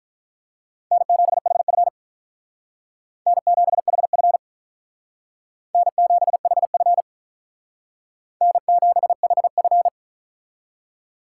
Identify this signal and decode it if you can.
Morse code, heard as a keyed tone, about 44 wpm, 700 Hz. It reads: N7HF N7HF N7HF N7HF